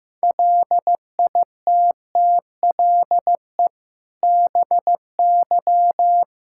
Morse code, heard as a keyed tone, about 15 wpm, 700 Hz. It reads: LITTLE BY
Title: Sentences from Top 1600 Words 15wpm